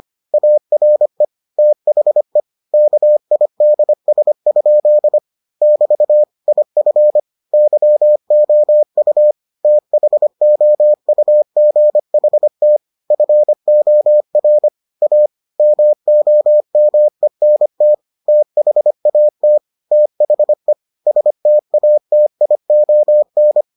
Morse code, heard as a keyed tone, about 25 wpm, 600 Hz. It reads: ARE THE KIDS? <BT> IF YOU THOUGHT FOR A MOMENT THAT THE STATION